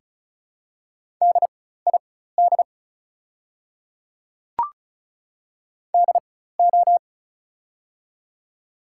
Morse code, heard as a keyed tone, about 35 wpm, spaced as for 16 wpm, 700 Hz. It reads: DID DO
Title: 4x Character Spacing - Top 100 Words - 35wpm